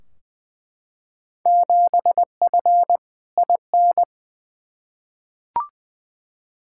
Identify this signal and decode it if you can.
Morse code, heard as a keyed tone, about 20 wpm, 700 Hz. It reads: 7F IN
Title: Field Day Exchanges Encore - 20wpm